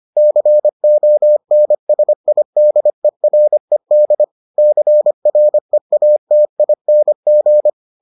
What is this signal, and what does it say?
Morse code, heard as a keyed tone, about 25 wpm, 600 Hz. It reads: CONSIDERED CREATING